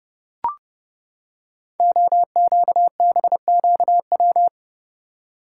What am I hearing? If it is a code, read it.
Morse code, heard as a keyed tone, about 30 wpm, 700 Hz. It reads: OQBQW
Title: Code Groups - 5 Letters - 30wpm